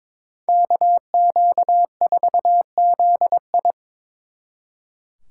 Morse code, heard as a keyed tone, about 25 wpm, 700 Hz. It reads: KQ4ZI
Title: Speed Racing - USA 2x2 Call Signs 22wpm